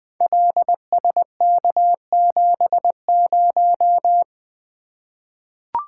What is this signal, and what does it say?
Morse code, heard as a keyed tone, about 20 wpm, 700 Hz. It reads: LSK70
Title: Code Groups - 5 Letter:Numbers - 20wpm